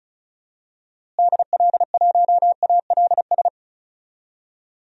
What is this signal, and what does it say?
Morse code, heard as a keyed tone, about 36 wpm, 700 Hz. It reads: DL1ALS
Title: DXCC-100 Call Signs - 6 Characters - Repeated 3x - 35wpm